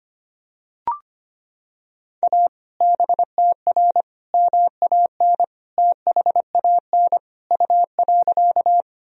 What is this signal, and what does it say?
Morse code, heard as a keyed tone, about 25 wpm, 700 Hz. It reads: A BTR MAN THAN U.
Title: Sentences from Top 400 Words and Abbreviations 25wpm